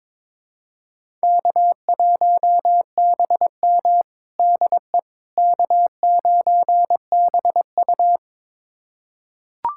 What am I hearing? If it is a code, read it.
Morse code, heard as a keyed tone, about 22 wpm, 700 Hz. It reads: K1BM DE K9BU
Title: USA Callsign DE Callsign - 4 Characters - 22wpm